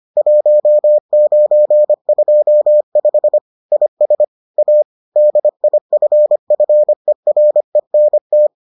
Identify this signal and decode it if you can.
Morse code, heard as a keyed tone, about 25 wpm, 600 Hz. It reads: 1925 IS A DIFFERENT